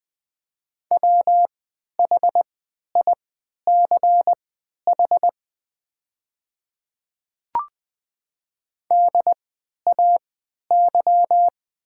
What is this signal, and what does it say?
Morse code, heard as a keyed tone, about 20 wpm, spaced as for 11 wpm, 700 Hz. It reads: WHICH DAY
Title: 3x Character Spacing - Top 100 Words - 20wpm